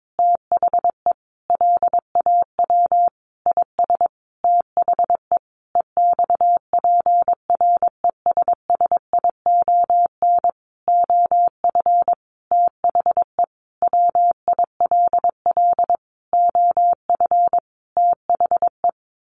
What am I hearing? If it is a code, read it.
Morse code, heard as a keyed tone, about 22 wpm, 700 Hz. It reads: THE LAW IS THE EXPRESSION OF THE WILL OF THE
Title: Sentences from Top 1500 Words 22wpm